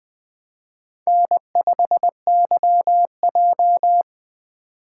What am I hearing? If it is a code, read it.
Morse code, heard as a keyed tone, about 20 wpm, 700 Hz. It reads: N5YJ
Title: Speed Racing - USA 1x2 Call Signs 17wpm